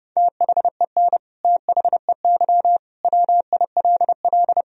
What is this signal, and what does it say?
Morse code, heard as a keyed tone, about 30 wpm, 700 Hz. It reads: THEN THEY WILL
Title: Sentences from Top 200 Words and Abbreviations 30wpm